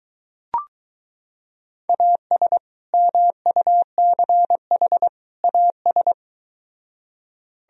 Morse code, heard as a keyed tone, about 22 wpm, 700 Hz. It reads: AS MUCH AS